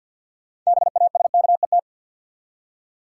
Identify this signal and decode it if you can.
Morse code, heard as a keyed tone, about 52 wpm, 700 Hz. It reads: BASKET
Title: Six Letter Words - 50wpm